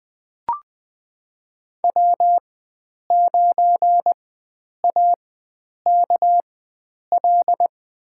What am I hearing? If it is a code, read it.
Morse code, heard as a keyed tone, about 20 wpm, spaced as for 9 wpm, 700 Hz. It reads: W9AKL
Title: USA Call Signs - 5 Characters - 4x Character Spacing - 20wpm